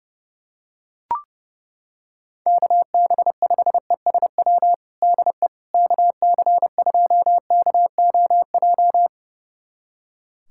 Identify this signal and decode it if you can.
Morse code, heard as a keyed tone, about 30 wpm, 700 Hz. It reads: KB5ESW DE KC2KOJ
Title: USA Callsign DE Callsign 30wpm